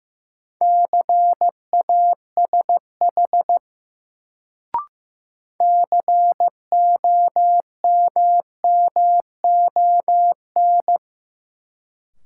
Morse code, heard as a keyed tone, about 15 wpm, 700 Hz. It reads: CASH COMMON
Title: Sentences from Top 700 Words - New Word Drill - 15wpm